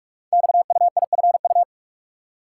Morse code, heard as a keyed tone, about 44 wpm, 690 Hz. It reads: XUIFU
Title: Code Groups - 5 Letters - 45wpm